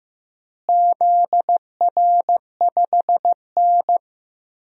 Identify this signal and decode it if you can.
Morse code, heard as a keyed tone, about 16 wpm, 700 Hz. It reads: ZR5N
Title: DXCC-100 Call Signs - 4 Characters - Repeated 3x - 15wpm